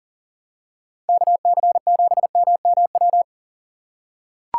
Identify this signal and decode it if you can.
Morse code, heard as a keyed tone, about 39 wpm, 700 Hz. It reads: KC7MMW
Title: USA Call Signs - 6 Characters - 2x Character Spacing - 40wpm